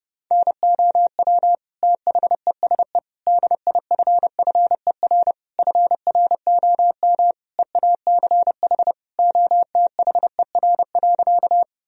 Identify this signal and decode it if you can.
Morse code, heard as a keyed tone, about 30 wpm, 700 Hz. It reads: NOW THESE DIFFER FROM EACH OTHER.
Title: Sentences from Top 1000 Words - Review Part 1 - 30wpm